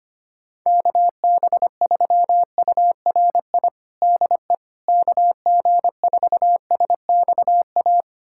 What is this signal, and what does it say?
Morse code, heard as a keyed tone, about 25 wpm, 700 Hz. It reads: KB3URI DE KG4SXA
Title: USA Callsign DE Callsign 25wpm